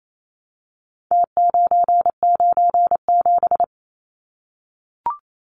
Morse code, heard as a keyed tone, about 28 wpm, 700 Hz. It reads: T997
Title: Numbers - Contest Style - 4-Digit 28wpm